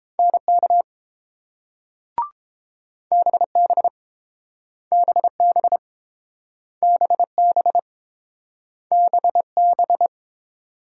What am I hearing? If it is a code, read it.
Morse code, heard as a keyed tone, about 30 wpm, 690 Hz. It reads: NK BB BB BB BB